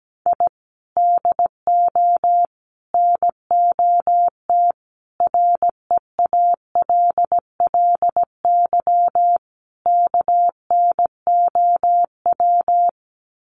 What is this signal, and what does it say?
Morse code, heard as a keyed tone, about 17 wpm, 700 Hz. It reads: I DO NOT REALLY KNOW
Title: Sets of 5 Words Encore 17wpm